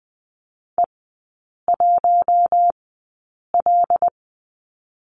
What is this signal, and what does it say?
Morse code, heard as a keyed tone, about 20 wpm, spaced as for 8 wpm, 700 Hz. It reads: E1L
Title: 081 Lesson 14e - Number 2 - Three Characters 20wpm